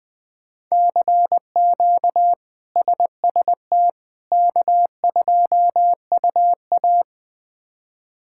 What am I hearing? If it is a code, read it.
Morse code, heard as a keyed tone, about 20 wpm, 700 Hz. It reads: CQ SST K2UA
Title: K1USN SST QSO Practice - 20wpm